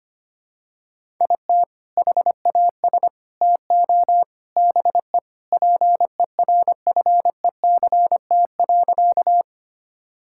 Morse code, heard as a keyed tone, about 25 wpm, 700 Hz. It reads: IT HAS TO BE PERFECT.
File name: practice / 25wpm / Sentences with 3-Word Phrases - 25wpm